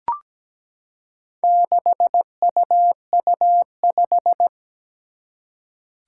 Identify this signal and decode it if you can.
Morse code, heard as a keyed tone, about 17 wpm, 700 Hz. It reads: 6UU5
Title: Four Challenging Letter-Number Combinations 17wpm